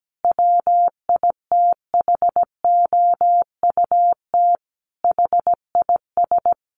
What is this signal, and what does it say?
Morse code, heard as a keyed tone, about 17 wpm, 700 Hz. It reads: WITHOUT HIS
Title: Sentences from Top 600 Words 17wpm